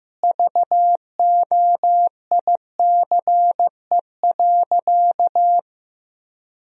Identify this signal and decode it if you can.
Morse code, heard as a keyed tone, about 15 wpm, 700 Hz. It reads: VOICE.
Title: Sentences from Top 1000 Words - Review Part 1 - 15wpm